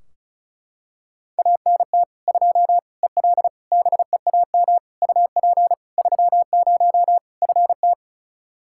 Morse code, heard as a keyed tone, about 34 wpm, 700 Hz. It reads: ANT 2 EL BEAM UP 30 FT
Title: Practice QSO - Exchange 2 - Info - 35wpm